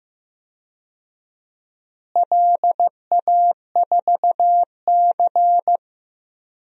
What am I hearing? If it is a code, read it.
Morse code, heard as a keyed tone, about 15 wpm, 700 Hz. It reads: LA4C
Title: DXCC-100 Call Signs - 4 Characters - Repeated 2x - 15wpm